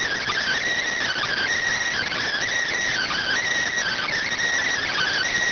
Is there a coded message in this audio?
Yes: Emergency Alert System (SAME) tones (header bursts) from 1.0 to 5.3 s, heard as harsh buzzing bursts.